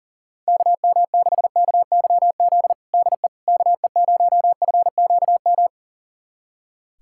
Morse code, heard as a keyed tone, about 40 wpm, 700 Hz. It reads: KM6KYZ DE KE0FQM
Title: USA Callsign DE Callsign 40wpm